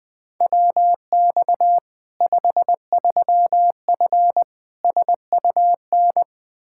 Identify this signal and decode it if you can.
Morse code, heard as a keyed tone, about 21 wpm, 700 Hz. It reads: WX 53F SUN
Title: Practice QSO - Exchange 2 - Info - 20wpm